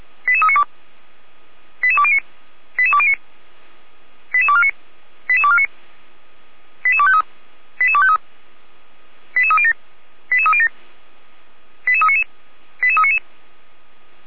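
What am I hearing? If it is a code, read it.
Five-tone selective call (ZVEI), heard as quick runs of five tones, tones of about 70 ms. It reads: 88392 88209 88209 88249 88249 88253 88253 88307 88307 88399 88399
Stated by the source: ZVEI five-tone selective call received by SDR.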